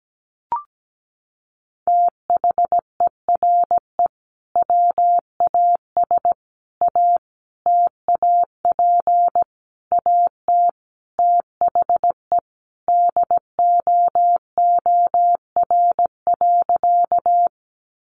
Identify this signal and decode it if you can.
Morse code, heard as a keyed tone, about 17 wpm, 700 Hz. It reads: THERE WAS A TAP AT THE DOOR.
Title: Sentences from Top 1400 Words 17wpm